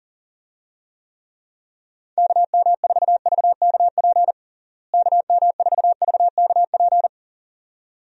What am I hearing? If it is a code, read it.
Morse code, heard as a keyed tone, about 40 wpm, 700 Hz. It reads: KM4VKP KM4VKP
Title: USA Call Signs - 6 Characters - Repeated 2x - 40wpm